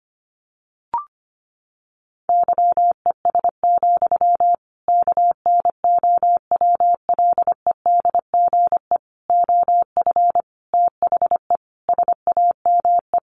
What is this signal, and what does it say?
Morse code, heard as a keyed tone, about 25 wpm, 700 Hz. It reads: YES, KNOWLEDGE OF THE SAME
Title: Sentences from Top 1500 Words 25wpm